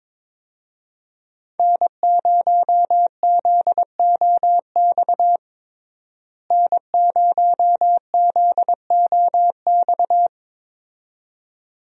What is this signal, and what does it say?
Morse code, heard as a keyed tone, about 22 wpm, 700 Hz. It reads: N0ZOX N0ZOX
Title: USA Call Signs - 5 Characters - Repeated 2x - 22wpm